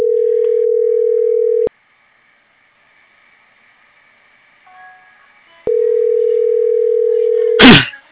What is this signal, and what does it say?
Ringback tone (North America), 440 + 480 Hz.